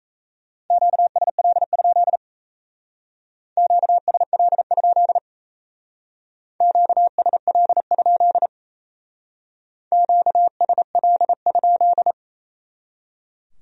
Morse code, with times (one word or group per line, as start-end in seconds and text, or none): 0.7-2.2 QSL?
3.6-5.2 QSL?
6.6-8.5 QSL?
9.9-12.1 QSL?